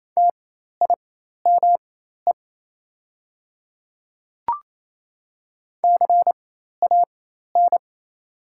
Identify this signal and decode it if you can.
Morse code, heard as a keyed tone, about 27 wpm, spaced as for 13 wpm, 690 Hz. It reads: TIME CAN
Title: 4x Character Spacing - Top 100 Words - 28wpm